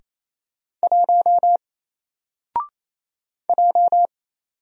Morse code, heard as a keyed tone, about 27 wpm, 700 Hz. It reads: 1 J
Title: Single Character (Letter:Number:Punctuation:Prosign) 28wpm